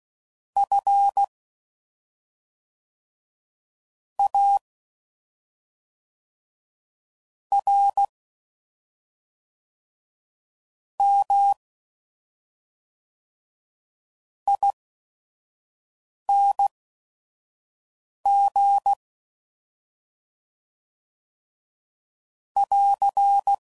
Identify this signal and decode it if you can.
Morse code, heard as a keyed tone, about 16 wpm, spaced as for 5 wpm, 800 Hz. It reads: F A R M ING <AR>
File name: Suffixes / Special ING's / SlowToFaster ING's